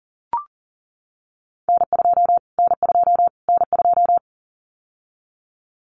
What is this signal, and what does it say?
Morse code, heard as a keyed tone, about 40 wpm, 700 Hz. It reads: N2 N2 N2
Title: Callsign Prefixes for Top 25 DXCC - Repeated 3x - 40wpm